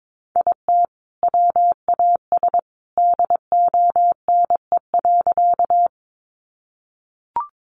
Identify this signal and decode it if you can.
Morse code, heard as a keyed tone, about 22 wpm, 700 Hz. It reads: IT WAS DONE.
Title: Sentences from Top 500 Words 22wpm